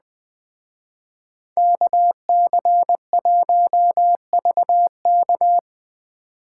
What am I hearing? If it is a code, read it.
Morse code, heard as a keyed tone, about 20 wpm, 700 Hz. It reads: KC1VK